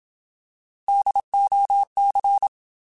Morse code, heard as a keyed tone, about 27 wpm, 800 Hz. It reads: DOC